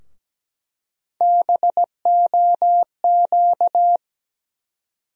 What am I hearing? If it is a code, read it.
Morse code, heard as a keyed tone, about 17 wpm, 700 Hz. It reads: BOQ
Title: USA Call Sign Postfixes - 3 Letters 17wpm